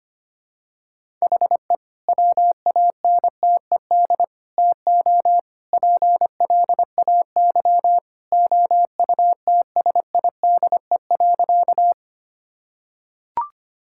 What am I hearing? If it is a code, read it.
Morse code, heard as a keyed tone, about 25 wpm, 700 Hz. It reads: HE WANTED TO PLAY OUTSIDE.
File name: practice / 25wpm / Sentences with 3-Word Phrases - 25wpm